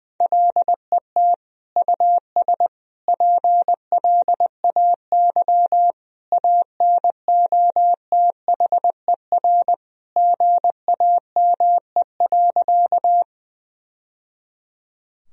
Morse code, heard as a keyed tone, about 20 wpm, 700 Hz. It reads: LET US PLAY ANOTHER GAME.
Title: Sentences from Top 1000 Words - Review Part 3 - 20wpm